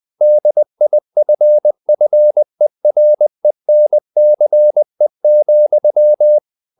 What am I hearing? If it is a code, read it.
Morse code, heard as a keyed tone, about 20 wpm, 600 Hz. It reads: DIFFERENCE,
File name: the old man and the sea / the old man and the sea part 1 20wpm